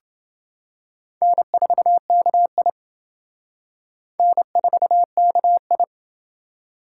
Morse code, heard as a keyed tone, about 27 wpm, 700 Hz. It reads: N4KI N4KI